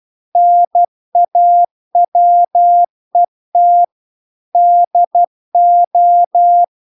Morse code, heard as a keyed tone, about 12 wpm, 700 Hz. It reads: NAWET DO